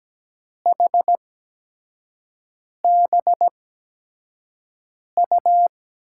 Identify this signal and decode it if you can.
Morse code, heard as a keyed tone, about 17 wpm, spaced as for 5 wpm, 700 Hz. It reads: HBU
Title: USA Call Signs - 6 Characters - 8x Character Spacing - 17wpm